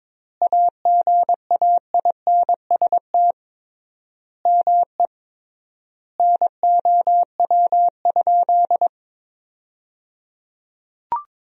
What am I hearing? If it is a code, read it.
Morse code, heard as a keyed tone, about 22 wpm, 700 Hz. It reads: AGAINST ME NOW?
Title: 3x Word-Spacing - Sentences from Top 400 Words - 22wpm